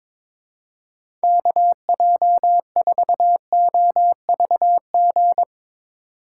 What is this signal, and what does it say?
Morse code, heard as a keyed tone, about 22 wpm, 700 Hz. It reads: KJ4OVG
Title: USA Call Signs - 6 Characters - 2x Character Spacing - 22wpm